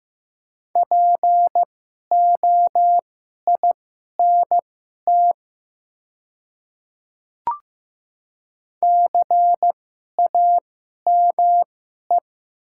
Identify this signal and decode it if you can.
Morse code, heard as a keyed tone, about 15 wpm, spaced as for 11 wpm, 700 Hz. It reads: POINT CAME
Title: 2x Character Spacing - Top 200 Words - 15wpm